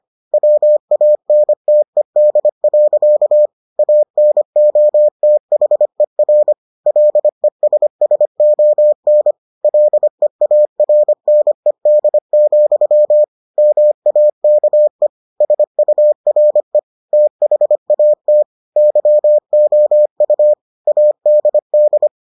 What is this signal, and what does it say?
Morse code, heard as a keyed tone, about 25 wpm, 600 Hz. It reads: WANTED. ANOTHER LESSON LEARNED, MAKE SURE THAT YOU ADD